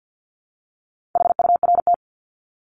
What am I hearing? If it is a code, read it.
Morse code, heard as a keyed tone, about 50 wpm, 700 Hz. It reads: HURT
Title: Top 1751-2000 Words 50wpm